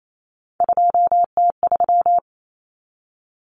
Morse code, heard as a keyed tone, about 28 wpm, 700 Hz. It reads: 2T3
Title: Common Cut Numbers - 3-Digit 28wpm